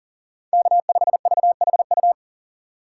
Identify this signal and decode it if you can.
Morse code, heard as a keyed tone, about 40 wpm, 700 Hz. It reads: K5VHU
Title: USA Call Signs - 5 Characters - 4x Character Spacing - 40wpm